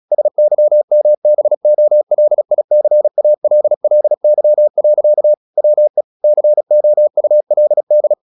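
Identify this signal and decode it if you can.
Morse code, heard as a keyed tone, about 36 wpm, 600 Hz. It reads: SYMBOLICALLY. WE COULD
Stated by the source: Morse practice at 36 wpm, text of Ken Follett's novel Never, from CW med Gnister.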